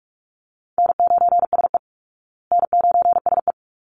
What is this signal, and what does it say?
Morse code, heard as a keyed tone, about 44 wpm, 700 Hz. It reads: N9SE N9SE
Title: Easy Call Signs Repeated 3x - 45wpm